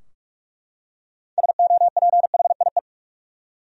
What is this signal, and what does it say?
Morse code, heard as a keyed tone, about 44 wpm, 700 Hz. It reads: SOPHIE
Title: USA Common Names - 45wpm